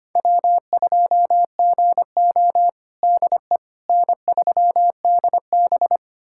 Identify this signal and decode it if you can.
Morse code, heard as a keyed tone, about 25 wpm, 700 Hz. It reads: W2GO DE N3DB
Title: USA Callsign DE Callsign - 4 Characters - 25wpm